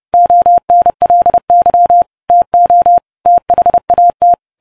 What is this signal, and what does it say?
Morse code, heard as a keyed tone, about 30 wpm, 700 Hz. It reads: ONLY TO THAT